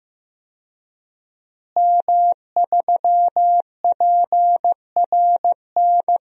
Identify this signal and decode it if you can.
Morse code, heard as a keyed tone, about 15 wpm, 700 Hz. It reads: M3PRN
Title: DXCC-100 Call Signs - 5 Characters - Repeated 2x - 15wpm